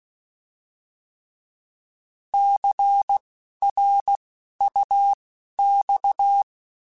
Morse code, heard as a keyed tone, about 16 wpm, spaced as for 12 wpm, 800 Hz. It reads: CRUX
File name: Alphabetical / Alphabetical mp3s / Cs @10